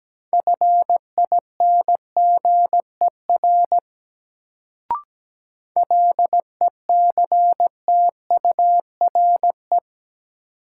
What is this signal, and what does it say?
Morse code, heard as a keyed tone, about 17 wpm, 700 Hz. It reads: FINGER LECTURE
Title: Sentences from Top 1600 Words - New Word Drill Alt - 17wpm